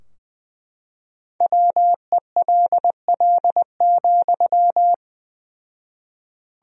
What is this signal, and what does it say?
Morse code, heard as a keyed tone, about 19 wpm, 700 Hz. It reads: WELL,
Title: 4x Word-Spacing - Sentences from Top 500 Words - 20wpm